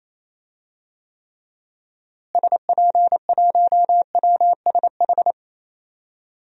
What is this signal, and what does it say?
Morse code, heard as a keyed tone, about 28 wpm, 700 Hz. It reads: SP1WSH
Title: DXCC-100 Call Signs - 6 Characters - Repeated 3x - 28wpm